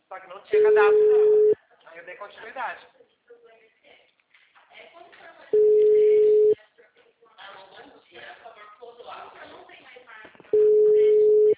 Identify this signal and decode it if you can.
Ringback tone (Europe, 425 Hz), with speech.